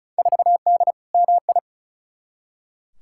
Morse code, heard as a keyed tone, about 36 wpm, 700 Hz. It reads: 4D MI
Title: Field Day Exchanges Encore - 35wpm